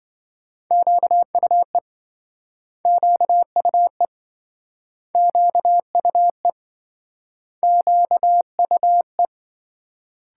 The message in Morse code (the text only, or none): QUE QUE QUE QUE